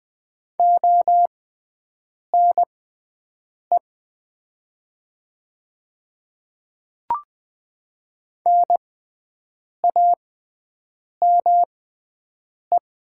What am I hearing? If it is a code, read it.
Morse code, heard as a keyed tone, about 20 wpm, spaced as for 7 wpm, 700 Hz. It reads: ONE NAME
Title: 6x Character Spacing - Top 200 Words - 20wpm